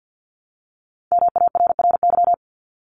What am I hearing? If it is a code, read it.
Morse code, heard as a keyed tone, about 50 wpm, 700 Hz. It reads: MARRY
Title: Multi-letter Phonemes Repeated 3x 50wpm